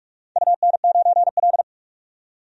Morse code, heard as a keyed tone, about 41 wpm, 700 Hz. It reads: UN9L
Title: DXCC-100 Call Signs - 4 Characters - 45wpm